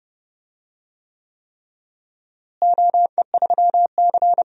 Morse code, heard as a keyed tone, about 29 wpm, 700 Hz. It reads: OE3C
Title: DXCC-100 Call Signs - 4 Characters - Repeated 3x - 30wpm